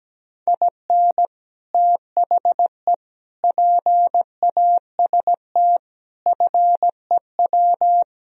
Morse code, heard as a keyed tone, about 17 wpm, 700 Hz. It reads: IN THE PAST FEW
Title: Speed-Racing - Sets of 5 Words 17wpm